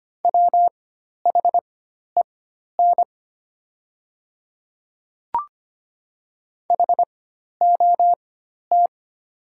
Morse code, heard as a keyed tone, about 25 wpm, spaced as for 12 wpm, 700 Hz. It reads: WHEN HOT
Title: 4x Character Spacing - Top 100 Words - 25wpm